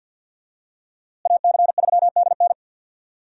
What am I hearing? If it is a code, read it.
Morse code, heard as a keyed tone, about 51 wpm, 700 Hz. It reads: AC3DN
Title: USA Call Signs - 5 Characters - Repeated 3x - 50wpm